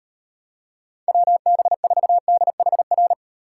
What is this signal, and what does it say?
Morse code, heard as a keyed tone, about 36 wpm, 700 Hz. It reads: WB4DHR